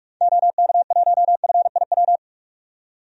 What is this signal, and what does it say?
Morse code, heard as a keyed tone, about 43 wpm, 700 Hz. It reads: OK1FIW
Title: DXCC-100 Call Signs - 6 Characters - 45wpm